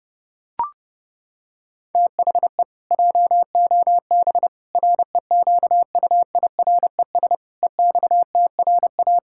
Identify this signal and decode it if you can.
Morse code, heard as a keyed tone, about 30 wpm, 700 Hz. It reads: THE JOB REQUIRES EXTRA
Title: Pangram Challenge v2 - 30wpm